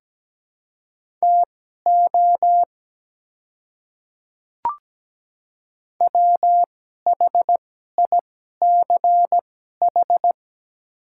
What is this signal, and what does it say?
Morse code, heard as a keyed tone, about 17 wpm, spaced as for 12 wpm, 700 Hz. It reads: TO WHICH
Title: 2x Character Spacing - Top 100 Words - 17wpm